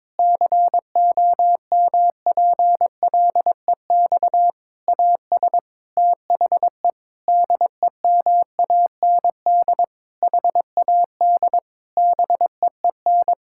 Morse code, heard as a keyed tone, about 22 wpm, 700 Hz. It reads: COMPLEX AS THE DEMAND HAD BEEN
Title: Sentences from Top 1000 Words - Review Part 6 - 22wpm